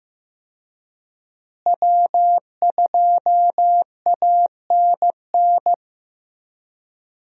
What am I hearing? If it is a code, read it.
Morse code, heard as a keyed tone, about 15 wpm, 700 Hz. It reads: W2ANN
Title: USA Call Signs - 5 Characters - Repeated 3x - 15wpm